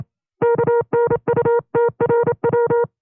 Morse code, heard as a keyed tone, about 27 wpm, 470 Hz. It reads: KNUTRW
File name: FreeFiles / CW-practice / 25WPM-side1